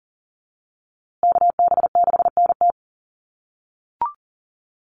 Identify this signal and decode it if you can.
Morse code, heard as a keyed tone, about 40 wpm, 700 Hz. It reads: KB6NT
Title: USA Call Signs Repeated 3x 40wpm